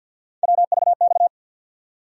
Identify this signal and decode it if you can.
Morse code, heard as a keyed tone, about 51 wpm, 700 Hz. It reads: WVX
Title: USA Call Sign Postfixes - 3 Letters 50wpm